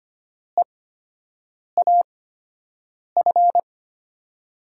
Morse code, heard as a keyed tone, about 25 wpm, spaced as for 7 wpm, 700 Hz. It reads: EAF
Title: USA Call Signs - 8x Character Spacing - 25wpm